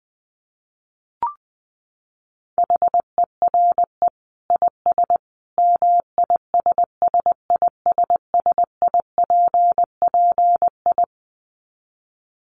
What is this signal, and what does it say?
Morse code, heard as a keyed tone, about 20 wpm, 700 Hz. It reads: HERE IS MISSISSIPPI
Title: Dit Challenge 20wpm